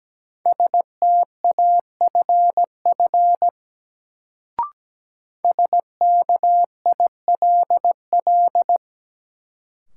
Morse code, heard as a keyed tone, about 17 wpm, 700 Hz. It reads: STAFF SKILL
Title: Sentences from Top 1200 Words - New Word Drill - 17wpm